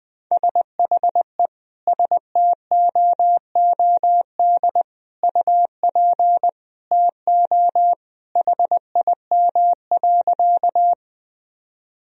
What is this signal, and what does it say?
Morse code, heard as a keyed tone, about 20 wpm, 700 Hz. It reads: SHE STOOD UP TO HIM.
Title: Sentences from Top 1000 Words - Review Part 2 - 20wpm